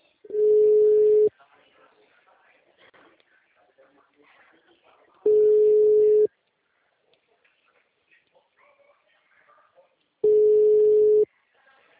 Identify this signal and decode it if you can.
Ringback tone (Europe, 425 Hz), with speech.